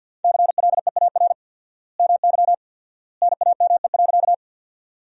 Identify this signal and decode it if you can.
Morse code, heard as a keyed tone, about 48 wpm, 700 Hz. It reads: CLEAR MY NAME.
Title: 4x Word-Spacing - Sentences from Top 500 Words - 50wpm